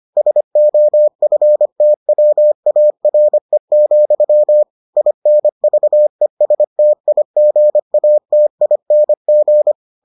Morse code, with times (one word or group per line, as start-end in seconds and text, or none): 0.2-4.6 SOFTWARE,
5.0-9.7 INVESTIGATING